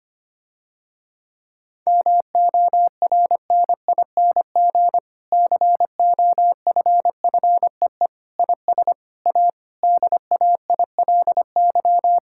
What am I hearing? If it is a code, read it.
Morse code, heard as a keyed tone, about 25 wpm, 700 Hz. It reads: MORNING COFFEE IS A DAILY
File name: practice / 25wpm / Difficult Sentences with Hints - 25wpm